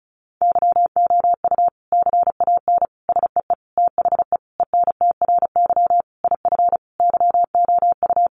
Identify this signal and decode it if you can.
Morse code, heard as a keyed tone, about 34 wpm, 700 Hz. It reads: YOU CAN SEE THE ENTRY IF YOU
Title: Sentences from Top 1500 Words 35wpm